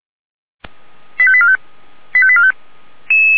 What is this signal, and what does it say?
Five-tone selective call (ZVEI), heard as quick runs of five tones, tones of about 70 ms. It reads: 85845 85845
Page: /464/SDR/ZVEI/